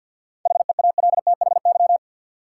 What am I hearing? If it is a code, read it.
Morse code, heard as a keyed tone, about 46 wpm, 700 Hz. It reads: HEALTHY
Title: Sentences from Top 1000 Words - Review Part 5 - New Word Drill - 50wpm